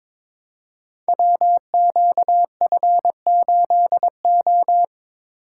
Morse code, heard as a keyed tone, about 22 wpm, 700 Hz. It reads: WQF8O
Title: Code Groups - 5 Letter:Numbers - 22wpm